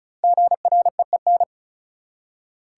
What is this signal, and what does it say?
Morse code, heard as a keyed tone, about 36 wpm, 700 Hz. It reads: GREEN
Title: Five Letter Words - 35wpm